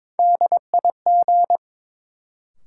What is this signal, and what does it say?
Morse code, heard as a keyed tone, about 22 wpm, 700 Hz. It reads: DIG
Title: Sentences from Top 1600 Words - New Word Drill - 22wpm